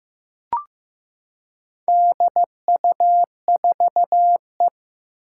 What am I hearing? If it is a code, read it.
Morse code, heard as a keyed tone, about 15 wpm, 700 Hz. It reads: DU4E
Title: Four Challenging Letter-Number Combinations 15wpm